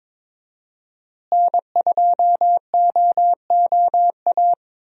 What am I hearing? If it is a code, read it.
Morse code, heard as a keyed tone, about 22 wpm, 700 Hz. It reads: N2OOA
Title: USA Call Signs - 5 Characters - Repeated 2x - 22wpm